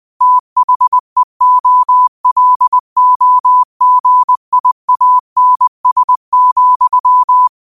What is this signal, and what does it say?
Morse code, heard as a keyed tone, about 20 wpm, 1000 Hz. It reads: THEOLOGIANS,
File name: qotdfiles / 20wpm